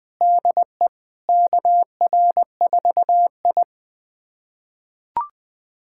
Morse code, heard as a keyed tone, about 20 wpm, 700 Hz. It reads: DE KR4I
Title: USA Callsign DE Callsign 20wpm